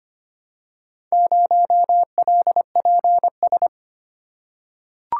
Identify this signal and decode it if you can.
Morse code, heard as a keyed tone, about 25 wpm, 700 Hz. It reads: 0LPS
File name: practice / 25wpm / Four Letter-Number Combinations - 25wpm